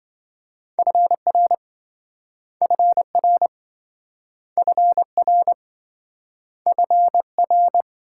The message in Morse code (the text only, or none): FR FR FR FR